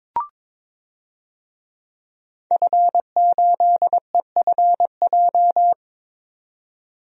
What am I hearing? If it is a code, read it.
Morse code, heard as a keyed tone, about 23 wpm, 700 Hz. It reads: F8EFJ
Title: DXCC-100 Call Signs - 5 Characters - 22wpm